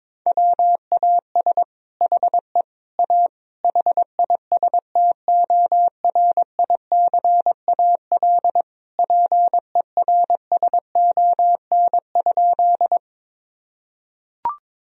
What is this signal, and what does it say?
Morse code, heard as a keyed tone, about 22 wpm, 700 Hz. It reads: WAS HE A HISTORICAL PERSON?
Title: Sentences from Top 1000 Words - Review Part 4 - 22wpm